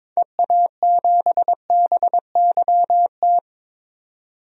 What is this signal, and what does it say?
Morse code, heard as a keyed tone, about 23 wpm, 700 Hz. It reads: EA7BYT
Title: DXCC-100 Call Signs - 6 Characters - 22wpm